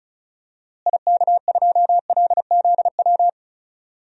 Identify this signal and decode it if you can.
Morse code, heard as a keyed tone, about 34 wpm, 700 Hz. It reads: IK2LZW
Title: DXCC-100 Call Signs - 6 Characters - Repeated 2x - 35wpm